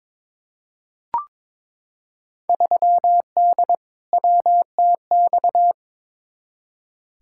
Morse code, heard as a keyed tone, about 22 wpm, 700 Hz. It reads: 3D WTX
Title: Field Day Exchanges Encore - 22wpm